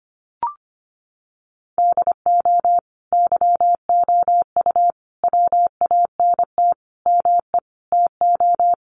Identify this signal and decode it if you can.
Morse code, heard as a keyed tone, about 25 wpm, 700 Hz. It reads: DO YOU WANT ME TO
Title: Sets of 5 Words 25wpm